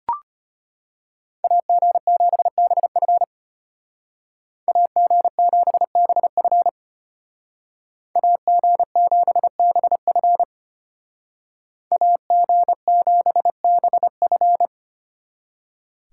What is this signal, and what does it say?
Morse code, heard as a keyed tone, about 30 wpm, 700 Hz. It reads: AG7BF AG7BF AG7BF AG7BF
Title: Speed Racing - USA 2x2 Call Signs 25wpm